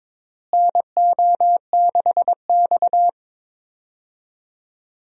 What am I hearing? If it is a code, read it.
Morse code, heard as a keyed tone, about 21 wpm, 700 Hz. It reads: NO6X
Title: USA Call Signs - 4 Characters - Repeated 3x - 22wpm